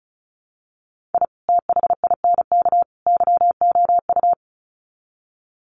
Morse code, heard as a keyed tone, about 34 wpm, 700 Hz. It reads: I THINK YOU
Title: Sets of 3 Words 35wpm